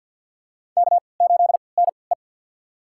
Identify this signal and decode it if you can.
Morse code, heard as a keyed tone, about 51 wpm, spaced as for 28 wpm, 700 Hz. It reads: K8NE
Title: USA Call Signs - 4 Characters - 3x Character Spacing - 50wpm